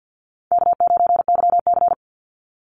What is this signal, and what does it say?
Morse code, heard as a keyed tone, about 52 wpm, 700 Hz. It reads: K9YC
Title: Easy Call Signs Repeated 3x - 50wpm